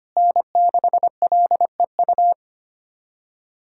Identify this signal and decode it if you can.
Morse code, heard as a keyed tone, about 25 wpm, 700 Hz. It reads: N6LEU
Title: USA Callsign DE Callsign - 5 Characters - 25wpm